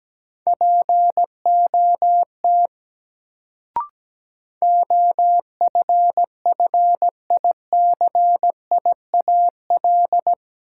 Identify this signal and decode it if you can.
Morse code, heard as a keyed tone, about 17 wpm, 700 Hz. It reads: POT OFFICIAL
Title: Sentences from Top 1000 Words - New Word Drill - 17wpm